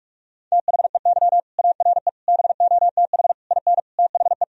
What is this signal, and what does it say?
Morse code, heard as a keyed tone, about 41 wpm, 700 Hz. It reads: THEY ARE BOTH IN THE
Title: Sentences from Top 1000 Words - Review Part 1 - 45wpm